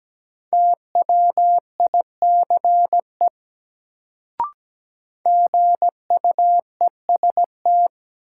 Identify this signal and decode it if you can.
Morse code, heard as a keyed tone, about 17 wpm, 700 Hz. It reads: TWICE GUEST
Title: Sentences from Top 1600 Words - New Word Drill - 17wpm